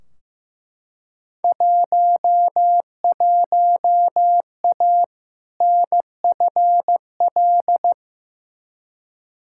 Morse code, heard as a keyed tone, about 15 wpm, 700 Hz. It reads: 11A NFL
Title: Field Day Exchanges Encore - 15wpm